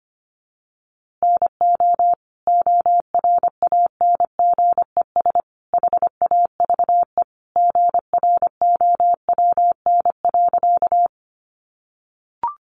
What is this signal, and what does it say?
Morse code, heard as a keyed tone, about 25 wpm, 700 Hz. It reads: NO ORANGES HAVE GROWN.
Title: Easy Phrases 25wpm